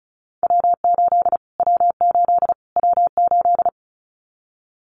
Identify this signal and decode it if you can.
Morse code, heard as a keyed tone, about 34 wpm, 700 Hz. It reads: W8 W8 W8
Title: Callsign Prefixes for Top 25 DXCC - Repeated 3x - 35wpm